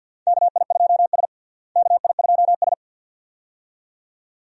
Morse code, heard as a keyed tone, about 52 wpm, 700 Hz. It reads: KI2S KI2S
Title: USA Call Signs - 4 Characters - Repeated 2x - 50wpm